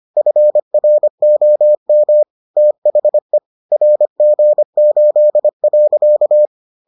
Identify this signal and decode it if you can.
Morse code, heard as a keyed tone, about 25 wpm, 600 Hz. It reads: FROM THE RG8.